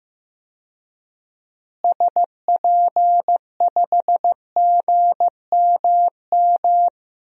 Morse code, heard as a keyed tone, about 15 wpm, 700 Hz. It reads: SP5GMM
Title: DXCC-100 Call Signs - 6 Characters - Repeated 2x - 15wpm